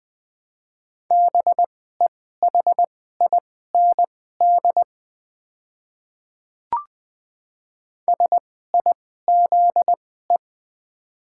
Morse code, heard as a keyed tone, about 21 wpm, spaced as for 15 wpm, 690 Hz. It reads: BEHIND SIZE
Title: 2x Character Spacing - Top 500 Words - 20wpm